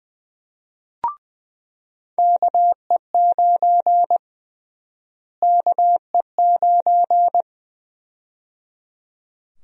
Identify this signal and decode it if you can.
Morse code, heard as a keyed tone, about 20 wpm, 700 Hz. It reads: KE9 KE9
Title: USA Call Sign Prefixes - Repeated 2x - 20wpm